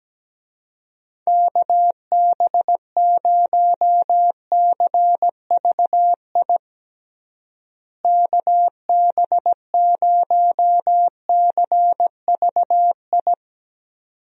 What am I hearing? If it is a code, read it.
Morse code, heard as a keyed tone, about 17 wpm, 700 Hz. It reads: KB0CVI KB0CVI
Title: USA Call Signs - 6 Characters - Repeated 2x - 17wpm